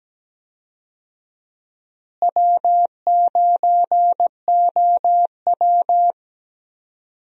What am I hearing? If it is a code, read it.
Morse code, heard as a keyed tone, about 17 wpm, 700 Hz. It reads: W9OW
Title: USA Call Signs - 4 Characters - Repeated 2x - 17wpm